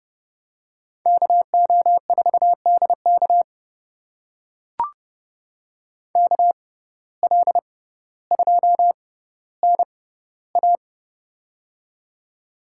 Morse code, heard as a keyed tone, about 29 wpm, spaced as for 15 wpm, 700 Hz. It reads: KO4DK KL2NA
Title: USA Call Signs - 5 Characters - 6x Character Spacing - 30wpm